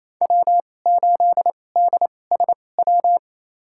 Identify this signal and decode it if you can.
Morse code, heard as a keyed tone, about 28 wpm, spaced as for 20 wpm, 700 Hz. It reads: W8DSW
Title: USA Call Signs - 5 Characters - 2x Character Spacing - 28wpm